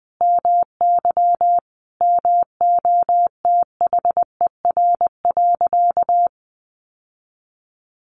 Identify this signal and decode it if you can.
Morse code, heard as a keyed tone, about 20 wpm, 700 Hz. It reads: MY MOTHER.
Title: Sentences from Top 1400 Words 20wpm